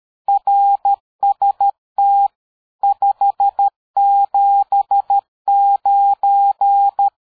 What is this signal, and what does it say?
Morse code, heard as a keyed tone, about 13 wpm, 800 Hz. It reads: RST 579
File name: dk5ke / QSO 60BpM